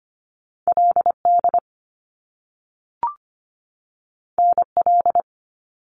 Morse code, heard as a keyed tone, about 25 wpm, 700 Hz. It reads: LD NL
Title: Top Two Letter Combinations in Words 25wpm